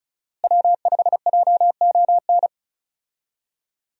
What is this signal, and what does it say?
Morse code, heard as a keyed tone, about 36 wpm, 700 Hz. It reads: W5JON
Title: USA Call Signs - 5 Characters - Repeated 2x - 35wpm